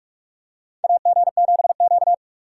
Morse code, heard as a keyed tone, about 45 wpm, 690 Hz. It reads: AG7Q
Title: USA Call Signs - 4 Characters - 8x Character Spacing - 45wpm